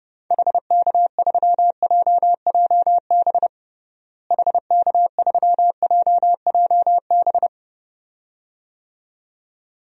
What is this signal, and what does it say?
Morse code, heard as a keyed tone, about 32 wpm, 700 Hz. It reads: HK3JJB HK3JJB
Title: DXCC-100 Call Signs - 6 Characters - Repeated 2x - 30wpm